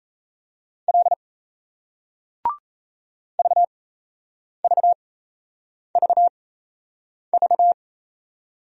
Morse code, heard as a keyed tone, about 38 wpm, spaced as for 8 wpm, 700 Hz. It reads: L VVVV